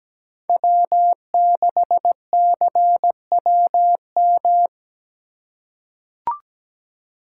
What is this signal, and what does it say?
Morse code, heard as a keyed tone, about 17 wpm, 700 Hz. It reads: W6CWM
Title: K1USN SST QSO Practice - 17wpm